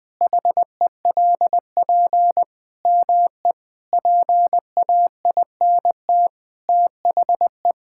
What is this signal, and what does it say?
Morse code, heard as a keyed tone, about 20 wpm, 700 Hz. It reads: HELP ME PAINT THE
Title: Sentences from Top 1000 Words - Review Part 6 - 20wpm